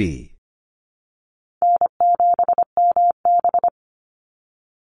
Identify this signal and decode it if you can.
Morse code, heard as a keyed tone, about 25 wpm, 700 Hz. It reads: N7MB
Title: Easy Call Signs Repeated 3x - 25wpm